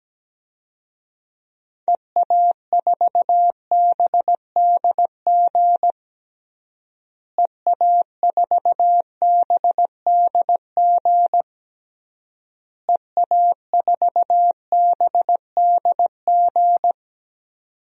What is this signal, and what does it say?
Morse code, heard as a keyed tone, about 17 wpm, 700 Hz. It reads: EA4BDG EA4BDG EA4BDG
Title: DXCC-100 Call Signs - 6 Characters - Repeated 3x - 17wpm